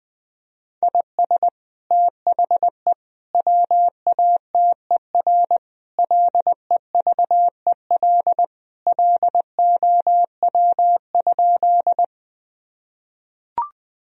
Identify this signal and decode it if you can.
Morse code, heard as a keyed tone, about 20 wpm, 700 Hz. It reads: IS THE WATER LEVEL LOW?
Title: Sentences from Top 1000 Words - Review Part 4 - 20wpm